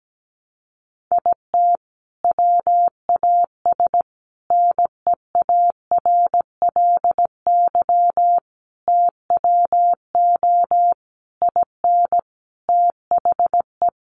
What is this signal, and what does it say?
Morse code, heard as a keyed tone, about 17 wpm, 700 Hz. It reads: IT WAS NEARLY TWO IN THE
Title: Sentences from Top 1300 Words 17wpm